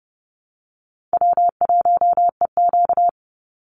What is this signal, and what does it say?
Morse code, heard as a keyed tone, about 30 wpm, 700 Hz. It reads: W1EQ
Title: Easy Call Signs Repeated 3x - 30wpm